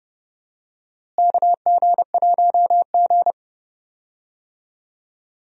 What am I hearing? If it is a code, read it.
Morse code, heard as a keyed tone, about 30 wpm, 700 Hz. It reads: KG1G
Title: USA Call Signs - 4 Characters - Repeated 2x - 30wpm